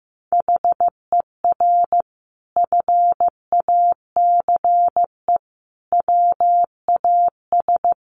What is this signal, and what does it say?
Morse code, heard as a keyed tone, about 15 wpm, 700 Hz. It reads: HER FACE WAS
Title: Sentences from Top 1200 Words 15wpm